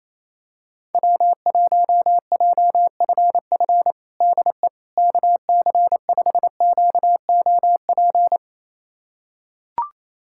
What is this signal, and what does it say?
Morse code, heard as a keyed tone, about 28 wpm, 700 Hz. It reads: W1JFF DE KC5QOP
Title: USA Callsign DE Callsign 28wpm